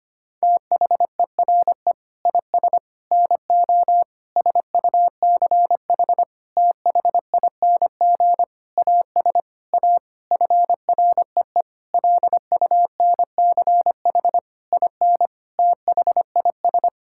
Morse code, heard as a keyed tone, about 25 wpm, 700 Hz. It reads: THERE IS NO SUCH THING AS A FREE LUNCH IN THIS
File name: practice / 25wpm / Sentences with 9-Word Phrases - 25wpm